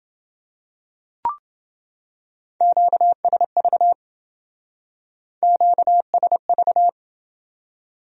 Morse code, heard as a keyed tone, about 28 wpm, 700 Hz. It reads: QSV QSV